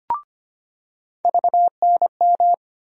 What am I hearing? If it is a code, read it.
Morse code, heard as a keyed tone, about 25 wpm, 700 Hz. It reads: VNM
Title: USA Call Sign Postfixes - 3 Letters 25wpm